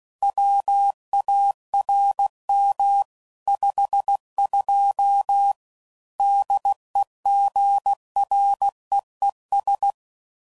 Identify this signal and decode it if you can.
Morse code, heard as a keyed tone, about 16 wpm, 800 Hz. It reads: WARM 52 DEGREES
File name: CW Code Groups @15WPM / Weather Sets / Weather 4 10